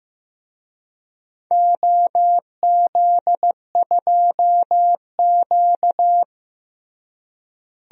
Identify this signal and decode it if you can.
Morse code, heard as a keyed tone, about 15 wpm, 700 Hz. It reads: OZ2Q
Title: DXCC-100 Call Signs - 4 Characters - Repeated 3x - 15wpm